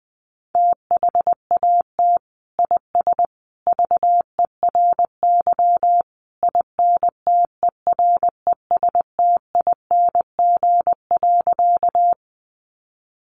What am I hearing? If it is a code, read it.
Morse code, heard as a keyed tone, about 20 wpm, 700 Hz. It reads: THAT IS VERY INTERESTING.
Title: Sentences from Top 1000 Words 20wpm